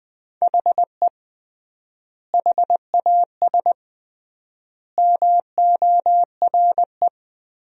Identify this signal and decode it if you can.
Morse code, heard as a keyed tone, about 20 wpm, 700 Hz. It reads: HE HAS MORE
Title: 3x Word-Spacing - Sentences from Top 200 Words - 20wpm